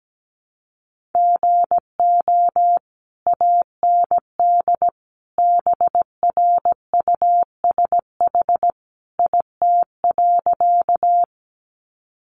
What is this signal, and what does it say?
Morse code, heard as a keyed tone, about 17 wpm, 700 Hz. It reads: GO AND BRUSH IT.
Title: Sentences from Top 1100 Words 17wpm